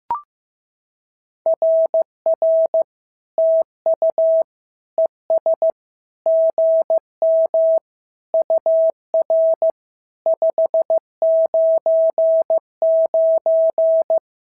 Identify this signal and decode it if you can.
Morse code, heard as a keyed tone, about 15 wpm, 650 Hz. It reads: RR TU ES GM UR 599
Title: SOTA QSO Practice 15wpm